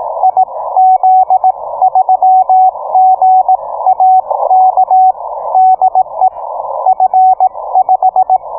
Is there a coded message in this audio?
Yes: Morse code from 0.2 to 8.4 s, heard as a keyed tone, about 18 wpm, spaced as for 15 wpm, 750 Hz. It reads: IZ3GAK DE F5